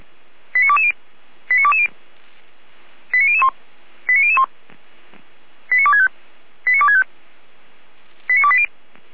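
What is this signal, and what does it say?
Five-tone selective call (ZVEI), heard as quick runs of five tones, tones of about 70 ms. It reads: 89220 89220 89001 89001 89265 89265 89280
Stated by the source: ZVEI five-tone selective call 88335, received on an SDR, from the feuerwehr-orlamuende fire brigade archive.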